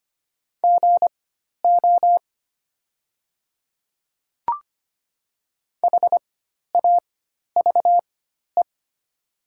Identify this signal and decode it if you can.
Morse code, heard as a keyed tone, about 25 wpm, spaced as for 12 wpm, 700 Hz. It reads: GO HAVE
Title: 4x Character Spacing - Top 200 Words - 25wpm